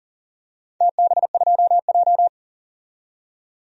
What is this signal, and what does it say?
Morse code, heard as a keyed tone, about 40 wpm, 700 Hz. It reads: TB2J
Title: DXCC-100 Call Signs - 4 Characters - 40wpm